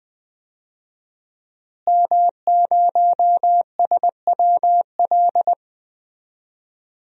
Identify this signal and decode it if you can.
Morse code, heard as a keyed tone, about 21 wpm, 700 Hz. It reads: M0SWL
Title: DXCC-100 Call Signs - 5 Characters - Repeated 3x - 20wpm